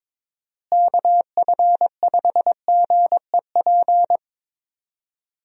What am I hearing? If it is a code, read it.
Morse code, heard as a keyed tone, about 22 wpm, 700 Hz. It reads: KF5GEP
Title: USA Call Signs - 6 Characters - Repeated 2x - 22wpm